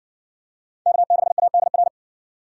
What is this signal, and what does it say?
Morse code, heard as a keyed tone, about 56 wpm, 700 Hz. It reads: K6ADR